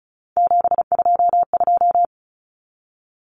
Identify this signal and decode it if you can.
Morse code, heard as a keyed tone, about 34 wpm, 700 Hz. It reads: 722
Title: Common Cut Numbers - 3-Digit 35wpm